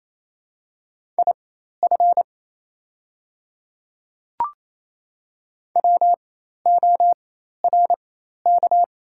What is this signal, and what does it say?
Morse code, heard as a keyed tone, about 28 wpm, spaced as for 13 wpm, 690 Hz. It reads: IF WORK
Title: 4x Character Spacing - Top 200 Words - 28wpm